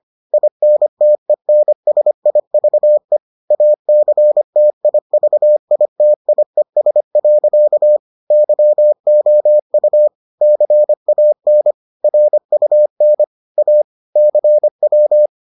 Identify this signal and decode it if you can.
Morse code, heard as a keyed tone, about 25 wpm, 600 Hz. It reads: INTENSIVE ACTIVITIES. YOU CAN RUN A CW